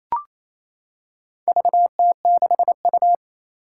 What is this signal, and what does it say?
Morse code, heard as a keyed tone, about 28 wpm, 700 Hz. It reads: VT6U
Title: Four Challenging Letter-Number Combinations 28wpm